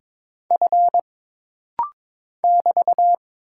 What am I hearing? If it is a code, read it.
Morse code, heard as a keyed tone, about 22 wpm, 700 Hz. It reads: F <BT>
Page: This rapid-fire timing is geared to push your ICR (Instant Character Recognition) skills.